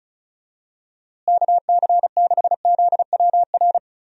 Morse code, heard as a keyed tone, about 36 wpm, 700 Hz. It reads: KC6ZWR
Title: USA Call Signs - 6 Characters - Repeated 3x - 35wpm